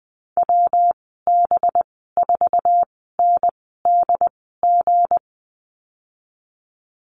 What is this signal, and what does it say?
Morse code, heard as a keyed tone, about 20 wpm, spaced as for 14 wpm, 700 Hz. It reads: WB4NDG
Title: 238 Lesson 38g - Letter q - Call Signs 20wpm